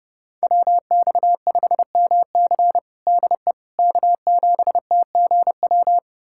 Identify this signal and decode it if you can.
Morse code, heard as a keyed tone, about 30 wpm, 700 Hz. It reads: WX5MC DE K7TGW
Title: USA Callsign DE Callsign - 5 Characters - 30wpm